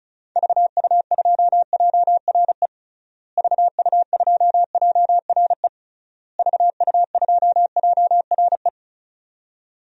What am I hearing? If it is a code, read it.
Morse code, heard as a keyed tone, about 34 wpm, 700 Hz. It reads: VU2JRE VU2JRE VU2JRE
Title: DXCC-100 Call Signs - 6 Characters - Repeated 3x - 35wpm